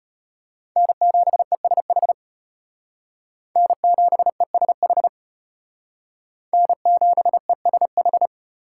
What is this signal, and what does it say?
Morse code, heard as a keyed tone, about 32 wpm, 700 Hz. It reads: N7ESH N7ESH N7ESH